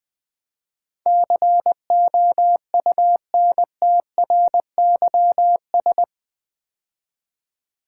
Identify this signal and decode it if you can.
Morse code, heard as a keyed tone, about 20 wpm, 700 Hz. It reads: COUNTRYS
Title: Build A Word - 4 Letters At A Time - 11-Letter Words - 20wpm